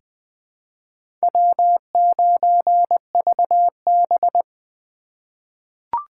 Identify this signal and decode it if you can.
Morse code, heard as a keyed tone, about 20 wpm, 700 Hz. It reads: W9VB
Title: USA Call Signs - 4 Characters - 2x Character Spacing - 20wpm